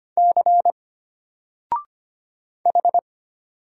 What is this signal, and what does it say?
Morse code, heard as a keyed tone, about 25 wpm, 700 Hz. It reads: C H
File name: Single-Letter-Number-25wpm